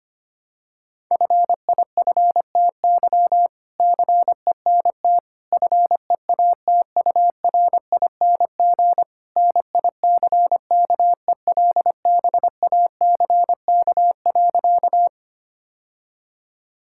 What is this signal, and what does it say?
Morse code, heard as a keyed tone, about 25 wpm, 700 Hz. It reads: FIFTY CENT FEATURING NICKELBACK.
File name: practice / 25wpm / All-Age Appropriate Jokes - 25wpm